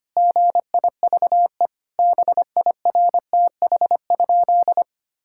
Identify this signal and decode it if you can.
Morse code, heard as a keyed tone, about 25 wpm, 700 Hz. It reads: GIVE BIRTH?
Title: Sentences from Top 1000 Words - Review Part 5 - 25wpm